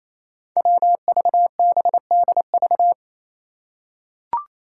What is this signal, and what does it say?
Morse code, heard as a keyed tone, about 28 wpm, 700 Hz. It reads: WVBDV
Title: Five Challenging Letter-Number Combinations 28wpm